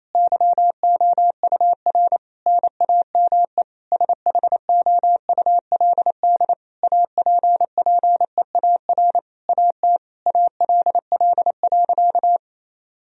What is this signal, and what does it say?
Morse code, heard as a keyed tone, about 28 wpm, 700 Hz. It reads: YOUR NAME SHOULD APPEAR AT ALL.
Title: Sentences from Top 1000 Words - Review Part 2 - 28wpm